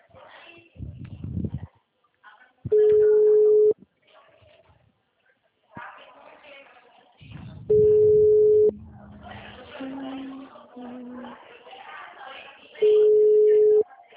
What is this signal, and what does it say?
Ringback tone (Europe, 425 Hz), with speech.
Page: Sorocaba